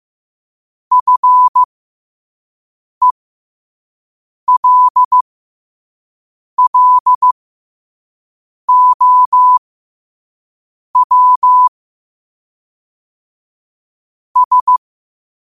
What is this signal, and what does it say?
Morse code, heard as a keyed tone, about 15 wpm, spaced as for 5 wpm, 1000 Hz. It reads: FELLOW S
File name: 5wpm